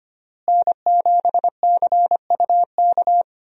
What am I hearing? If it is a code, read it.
Morse code, heard as a keyed tone, about 25 wpm, 700 Hz. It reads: N7CUK
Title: USA Call Signs - 5 Characters - Repeated 3x - 25wpm